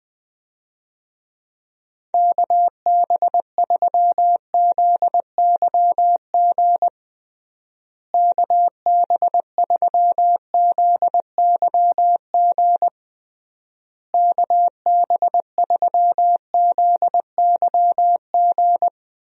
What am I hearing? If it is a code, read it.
Morse code, heard as a keyed tone, about 20 wpm, 700 Hz. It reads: KB3ZYG KB3ZYG KB3ZYG
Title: USA Call Signs - 6 Characters - Repeated 3x - 20wpm